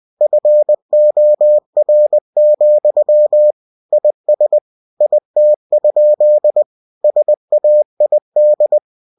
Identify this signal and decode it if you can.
Morse code, heard as a keyed tone, about 20 wpm, 600 Hz. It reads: FOR, IS IT? SAID
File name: cw-books / winnie-the-pooh / winnie-20wpm-page-0102